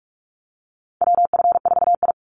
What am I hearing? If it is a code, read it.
Morse code, heard as a keyed tone, about 44 wpm, 700 Hz. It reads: WF4I
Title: NA SOTA Chaser Callsigns 45wpm